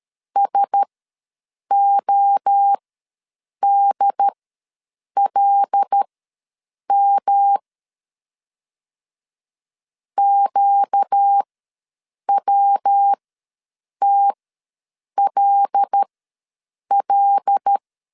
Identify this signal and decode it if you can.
Morse code, heard as a keyed tone, about 13 wpm, spaced as for 7 wpm, 800 Hz. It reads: SODLM QWTLL
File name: LEZIONI / 5 LBVC / 5 L ws 12wpm 2958